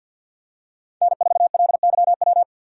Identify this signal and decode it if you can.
Morse code, heard as a keyed tone, about 48 wpm, 700 Hz. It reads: N4LYW
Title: USA Call Signs - 5 Characters - 50wpm